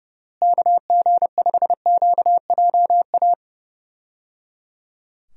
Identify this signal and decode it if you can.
Morse code, heard as a keyed tone, about 30 wpm, 700 Hz. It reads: KG5QJA
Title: USA Call Signs - 6 Characters - Repeated 3x - 30wpm